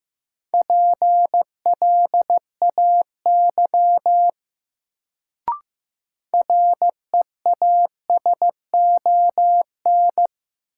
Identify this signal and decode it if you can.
Morse code, heard as a keyed tone, about 15 wpm, 700 Hz. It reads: PLAY REASON
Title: Sets of 5 Words Encore - New Word Drill - 15wpm